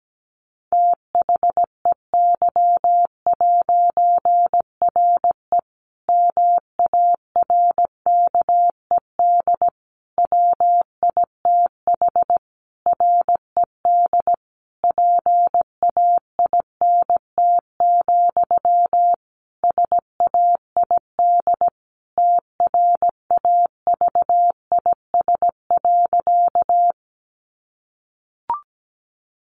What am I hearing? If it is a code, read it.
Morse code, heard as a keyed tone, about 17 wpm, 700 Hz. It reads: THEY'RE MARKED WITH RED PAINT, SAID TRAVIS.
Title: A Sound Of Thunder 17wpm